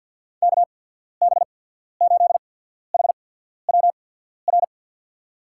Morse code, heard as a keyed tone, about 54 wpm, 700 Hz. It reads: K B 8 H W R
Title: USA Call Signs - 6 Characters - 8x Character Spacing - 50wpm